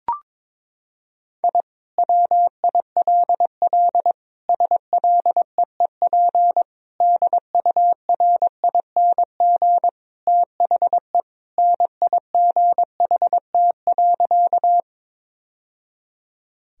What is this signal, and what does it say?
Morse code, heard as a keyed tone, about 22 wpm, 700 Hz. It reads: I WILL SLEEP DURING THE NIGHT.
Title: Sentences from Top 1000 Words - Review Part 2 - 22wpm